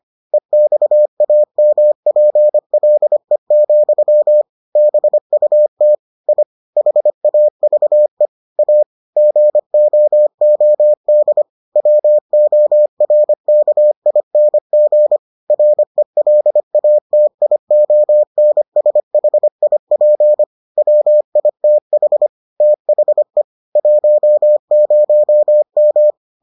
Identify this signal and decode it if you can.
Morse code, heard as a keyed tone, about 25 wpm, 600 Hz. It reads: EXAMPLE, BUT I HAVE A GOOD WORKING RELATIONSHIP WITH THE 10M